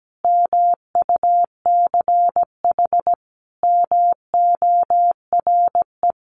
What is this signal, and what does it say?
Morse code, heard as a keyed tone, about 17 wpm, 700 Hz. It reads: MUCH MORE
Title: Sentences from Top 1200 Words 17wpm